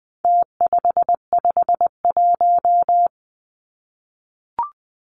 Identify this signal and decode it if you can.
Morse code, heard as a keyed tone, about 20 wpm, 700 Hz. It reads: T551
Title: Numbers - Contest Style - 4-Digit 20wpm